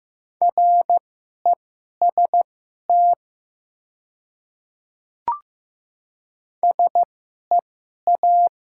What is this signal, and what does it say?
Morse code, heard as a keyed tone, about 15 wpm, spaced as for 11 wpm, 700 Hz. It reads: REST SEA
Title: 2x Character Spacing - Top 500 Words - 15wpm